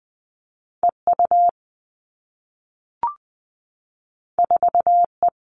Morse code, heard as a keyed tone, about 20 wpm, 700 Hz. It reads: EU 4E
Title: All Cut Numbers - 2-Digit 20wpm